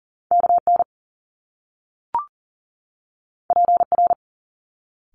Morse code, heard as a keyed tone, about 40 wpm, 700 Hz. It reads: KN PR
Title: Top Two Letter Combinations in Words 40wpm